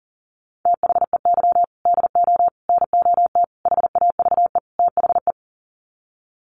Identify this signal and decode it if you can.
Morse code, heard as a keyed tone, about 40 wpm, 700 Hz. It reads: THEY DO NOT HAVE THE
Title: Sets of 5 Words 40wpm